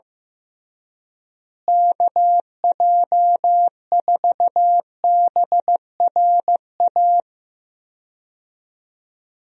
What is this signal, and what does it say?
Morse code, heard as a keyed tone, about 16 wpm, 700 Hz. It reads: KJ4BRA
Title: USA Call Signs - 6 Characters - Repeated 3x - 15wpm